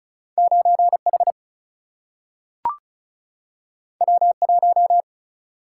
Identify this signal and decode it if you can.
Morse code, heard as a keyed tone, about 36 wpm, 700 Hz. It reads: 9 H W 1
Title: Single-Letter Single-Number 35wpm